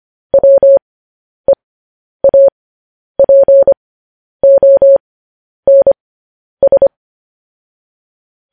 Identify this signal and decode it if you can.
Morse code, heard as a keyed tone, about 25 wpm, spaced as for 10 wpm, 550 Hz. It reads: WEAPONS
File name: podcast / News headlines-1x-25x10